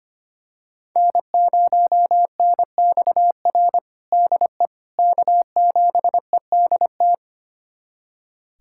Morse code, heard as a keyed tone, about 25 wpm, 700 Hz. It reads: N0NXR DE K7EDT
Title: USA Callsign DE Callsign - 5 Characters - 25wpm